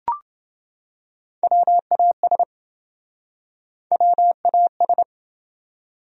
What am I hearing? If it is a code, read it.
Morse code, heard as a keyed tone, about 30 wpm, 700 Hz. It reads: WAS WAS